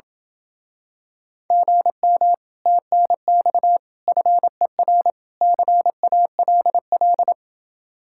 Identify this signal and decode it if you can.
Morse code, heard as a keyed tone, about 28 wpm, 700 Hz. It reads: GM TNX FER CALL